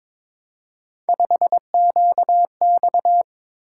Morse code, heard as a keyed tone, about 22 wpm, 700 Hz. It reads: 5QX
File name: practice / 22wpm / Three Letter-Number Combinations - 22wpm